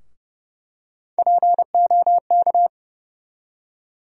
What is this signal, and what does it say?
Morse code, heard as a keyed tone, about 32 wpm, 700 Hz. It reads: POK
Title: USA Call Sign Postfixes - 3 Letters 30wpm